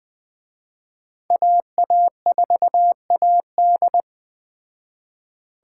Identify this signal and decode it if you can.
Morse code, heard as a keyed tone, about 19 wpm, 700 Hz. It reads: AA4AD